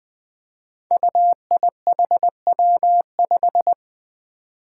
Morse code, heard as a keyed tone, about 21 wpm, 700 Hz. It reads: UIHW5
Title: Five Challenging Letter-Number Combinations 20wpm